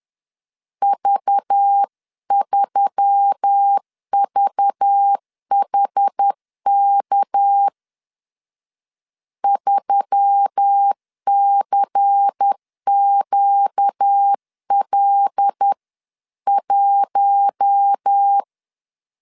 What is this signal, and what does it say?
Morse code, heard as a keyed tone, about 10 wpm, 800 Hz. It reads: V3VHK 3CQL1